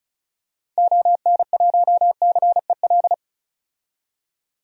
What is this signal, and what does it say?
Morse code, heard as a keyed tone, about 36 wpm, 700 Hz. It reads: ON1CEL
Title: DXCC-100 Call Signs - 6 Characters - 35wpm